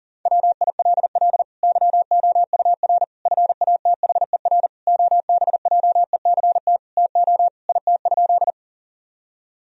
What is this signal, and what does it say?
Morse code, heard as a keyed tone, about 40 wpm, 700 Hz. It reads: WILL YOUR FATHER OBJECT TO IT?
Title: Sentences from Top 1000 Words - Review Part 3 - 40wpm